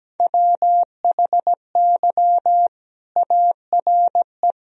Morse code, heard as a keyed tone, about 17 wpm, 700 Hz. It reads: WHY ARE
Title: Sentences from Top 1000 Words - Review Part 2 - 17wpm